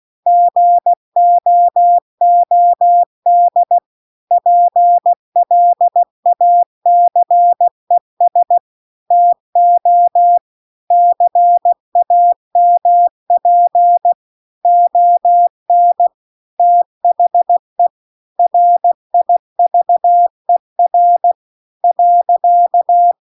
Morse code, heard as a keyed tone, about 16 wpm, 700 Hz. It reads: GOOD PLACES TO CAMP ON THE RIVER .